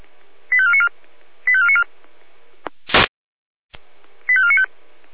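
Five-tone selective call (ZVEI), heard as quick runs of five tones, tones of about 70 ms. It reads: 85494 85494 85495
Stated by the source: Five-tone ZVEI selective call 85491, received by SDR.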